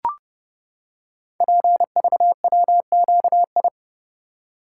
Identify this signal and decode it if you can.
Morse code, heard as a keyed tone, about 30 wpm, 700 Hz. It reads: PVWQI
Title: Code Groups - 5 Letters - 30wpm